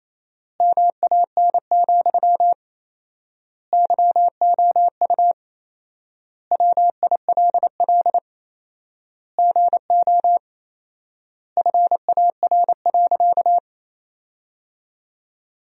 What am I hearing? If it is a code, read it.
Morse code, heard as a keyed tone, about 28 wpm, 700 Hz. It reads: MAN, YOU WILL GO FAR.
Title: 4x Word-Spacing - Sentences from Top 400 Words - 28wpm